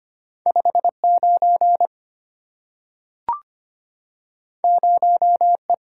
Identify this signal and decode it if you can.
Morse code, heard as a keyed tone, about 25 wpm, 700 Hz. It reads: 59 0E
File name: Two Letter-Number Combinations - 25wpm